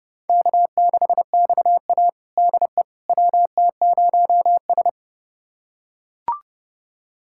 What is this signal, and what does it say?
Morse code, heard as a keyed tone, about 30 wpm, 700 Hz. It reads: K6XA DE WT0S
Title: USA Callsign DE Callsign - 4 Characters - 30wpm